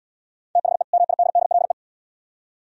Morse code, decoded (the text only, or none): THE MIDDLE